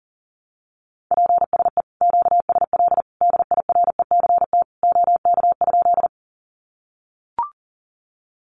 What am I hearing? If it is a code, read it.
Morse code, heard as a keyed tone, about 37 wpm, 700 Hz. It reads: PSE QSL DIRECT OK?
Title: Common QSO Phrases 40wpm